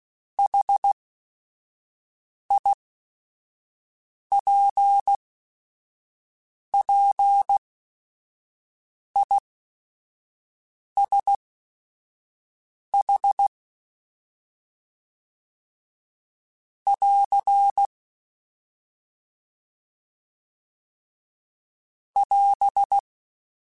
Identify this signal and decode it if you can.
Morse code, heard as a keyed tone, about 16 wpm, spaced as for 5 wpm, 800 Hz. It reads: HIPPISH <AR> <AS>
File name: General / General-mp3s / ISH @5